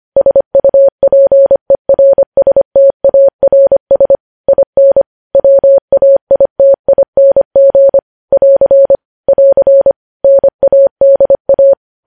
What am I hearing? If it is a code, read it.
Morse code, heard as a keyed tone, about 25 wpm, 550 Hz. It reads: SUPERSTARS IN WAITING <AR> <AR> NADA
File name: podcast / News headlines-1x-25x25